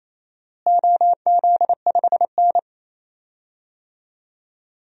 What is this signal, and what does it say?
Morse code, heard as a keyed tone, about 28 wpm, 700 Hz. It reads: OZ5N
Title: DXCC-100 Call Signs - 4 Characters - Repeated 3x - 28wpm